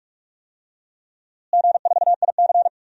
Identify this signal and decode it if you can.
Morse code, heard as a keyed tone, about 46 wpm, 700 Hz. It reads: G4IC
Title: DXCC-100 Call Signs - 4 Characters - Repeated 3x - 45wpm